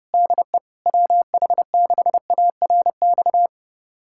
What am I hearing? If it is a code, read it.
Morse code, heard as a keyed tone, about 30 wpm, 700 Hz. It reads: DE WH6ARX
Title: USA Callsign DE Callsign - 6 Characters - 30wpm